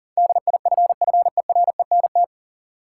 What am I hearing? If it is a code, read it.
Morse code, heard as a keyed tone, about 40 wpm, 700 Hz. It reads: DIFFERENT
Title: Sentences from Top 600 Words - New Word Drill - 40wpm